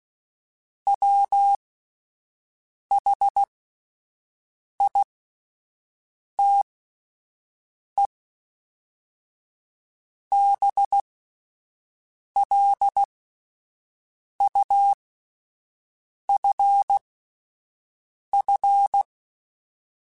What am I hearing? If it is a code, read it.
Morse code, heard as a keyed tone, about 16 wpm, spaced as for 5 wpm, 800 Hz. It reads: WHITE BLUFF